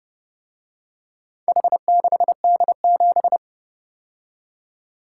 Morse code, heard as a keyed tone, about 30 wpm, 700 Hz. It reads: H6D7